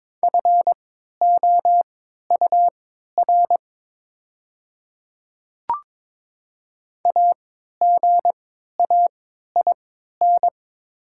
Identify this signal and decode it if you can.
Morse code, heard as a keyed tone, about 22 wpm, spaced as for 13 wpm, 700 Hz. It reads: FOUR AGAIN
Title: 3x Character Spacing - Top 500 Words - 22wpm